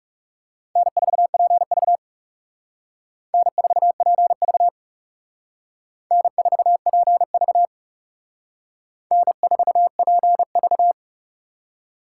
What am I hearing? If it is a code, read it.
Morse code, heard as a keyed tone, about 37 wpm, 700 Hz. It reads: N4PV N4PV N4PV N4PV
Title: Speed Racing - USA 1x2 Call Signs 30wpm